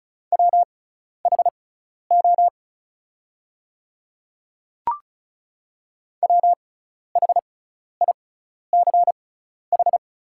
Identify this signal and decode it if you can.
Morse code, heard as a keyed tone, about 36 wpm, spaced as for 12 wpm, 700 Hz. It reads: WHO WHICH
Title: 6x Character Spacing - Top 100 Words - 35wpm